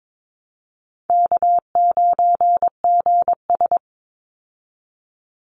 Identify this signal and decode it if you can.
Morse code, heard as a keyed tone, about 22 wpm, 700 Hz. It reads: K9GS
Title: Call Signs Repeated 3x - 22wpm